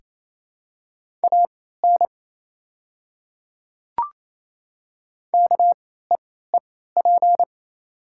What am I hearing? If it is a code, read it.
Morse code, heard as a keyed tone, about 27 wpm, spaced as for 16 wpm, 700 Hz. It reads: AN KEEP
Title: 3x Character Spacing - Top 300 Words - 28wpm